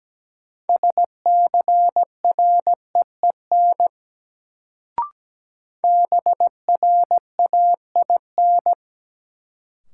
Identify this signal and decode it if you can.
Morse code, heard as a keyed tone, about 17 wpm, 700 Hz. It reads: SCREEN BRAIN
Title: Sentences from Top 1000 Words - Review Part 6 - New Word Drill - 17wpm